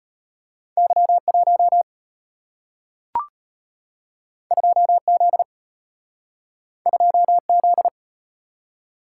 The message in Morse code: Y1 2Z 2Z

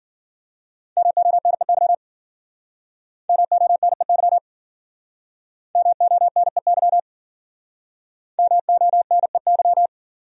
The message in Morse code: MONEY MONEY MONEY MONEY

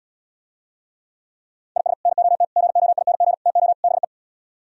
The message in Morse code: IS NOT GOING TO BE

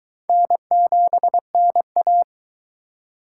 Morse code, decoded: N7NA

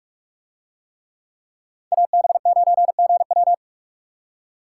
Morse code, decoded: AB9GW

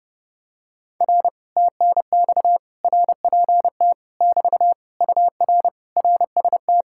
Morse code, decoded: R TNX RPT <BT> UR RST

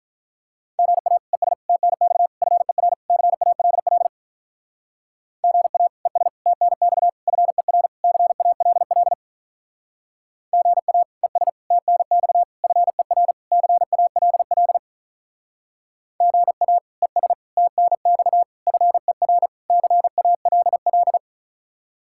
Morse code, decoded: GA ES TNX FER CALL GA ES TNX FER CALL GA ES TNX FER CALL GA ES TNX FER CALL

